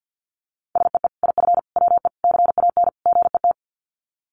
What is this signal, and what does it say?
Morse code, heard as a keyed tone, about 50 wpm, 700 Hz. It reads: SEE IF WE CAN GET
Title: Sets of 5 Words Encore2 50wpm